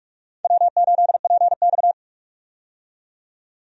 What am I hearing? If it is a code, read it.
Morse code, heard as a keyed tone, about 45 wpm, 700 Hz. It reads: W8PX
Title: USA Call Signs - 4 Characters - 45wpm